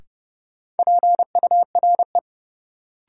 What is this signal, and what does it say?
Morse code, heard as a keyed tone, about 30 wpm, 700 Hz. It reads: PURE
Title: Sentences from Top 1500 Words - New Word Drill - 30wpm